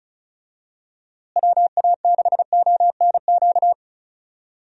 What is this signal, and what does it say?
Morse code, heard as a keyed tone, about 34 wpm, 700 Hz. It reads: WA6ONQ
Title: USA Call Signs - 6 Characters - Repeated 2x - 35wpm